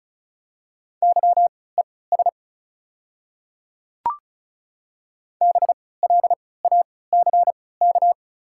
Morse code, heard as a keyed tone, about 34 wpm, spaced as for 20 wpm, 690 Hz. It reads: YES BLACK
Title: 3x Character Spacing - Top 500 Words - 35wpm